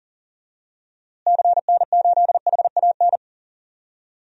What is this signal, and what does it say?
Morse code, heard as a keyed tone, about 40 wpm, 700 Hz. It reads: CN8HAN
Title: DXCC-100 Call Signs - 6 Characters - Repeated 2x - 40wpm